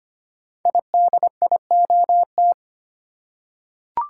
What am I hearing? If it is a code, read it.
Morse code, heard as a keyed tone, about 25 wpm, 700 Hz. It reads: IDIOT
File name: 5 Letter Words - First 3 Letters Given - 25wpm